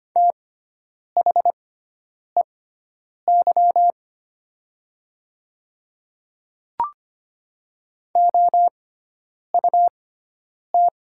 Morse code, heard as a keyed tone, about 25 wpm, spaced as for 9 wpm, 700 Hz. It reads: THEY OUT
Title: 6x Character Spacing - Top 100 Words - 25wpm